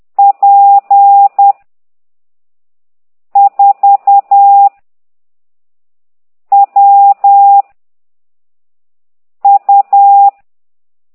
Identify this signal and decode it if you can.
Morse code, heard as a keyed tone, about 10 wpm, 800 Hz. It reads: P 4 W U